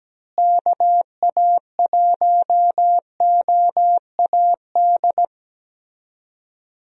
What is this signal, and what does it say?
Morse code, heard as a keyed tone, about 17 wpm, 700 Hz. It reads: KA1OAD
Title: USA Call Signs - 6 Characters - Repeated 3x - 17wpm